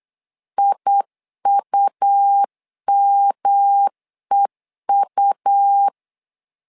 Morse code, heard as a keyed tone, about 8 wpm, 800 Hz. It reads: IUMEU
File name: LEZIONI / 3 MWRO / 3 M 8wpm 2300